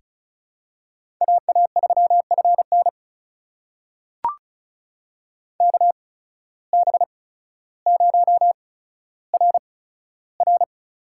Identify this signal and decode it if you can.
Morse code, heard as a keyed tone, about 36 wpm, spaced as for 15 wpm, 700 Hz. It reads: AA3FN KB0RR
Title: USA Call Signs - 8x Character Spacing - 35wpm